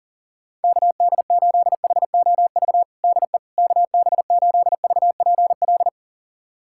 Morse code, heard as a keyed tone, about 40 wpm, 700 Hz. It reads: KD8HOV DE KB8VPL
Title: USA Callsign DE Callsign - 6 Characters - 40wpm